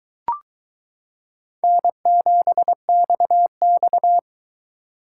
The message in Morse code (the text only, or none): N7XX